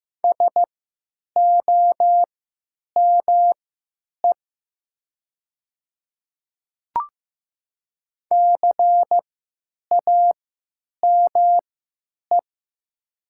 Morse code, heard as a keyed tone, about 15 wpm, spaced as for 9 wpm, 700 Hz. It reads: SOME CAME
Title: 3x Character Spacing - Top 300 Words - 15wpm